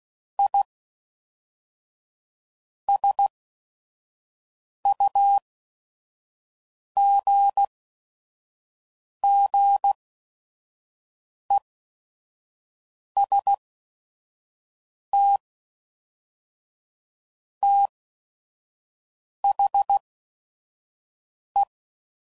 Morse code, heard as a keyed tone, about 16 wpm, spaced as for 5 wpm, 800 Hz. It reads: I SUGGEST THE
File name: CW Code Groups @5WPM / Technical Subjects / AND Gate